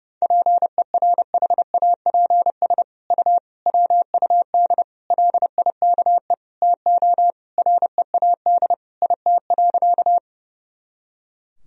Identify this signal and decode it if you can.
Morse code, heard as a keyed tone, about 30 wpm, 700 Hz. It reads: PERHAPS U WUD LIKE TO READ IT.
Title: Sentences from Top 500 Words and Abbreviations 30wpm